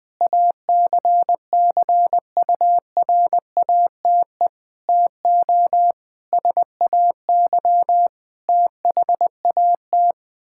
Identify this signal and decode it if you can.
Morse code, heard as a keyed tone, about 20 wpm, 700 Hz. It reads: ACCURATE TO SAY THAT
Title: Sets of 8 Words - 20wpm